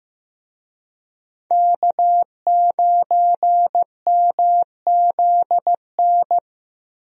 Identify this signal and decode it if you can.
Morse code, heard as a keyed tone, about 15 wpm, 700 Hz. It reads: K9MZN
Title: USA Call Signs - 5 Characters - Repeated 3x - 15wpm